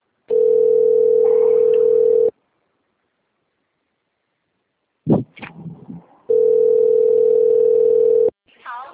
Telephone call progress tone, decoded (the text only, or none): ringback tone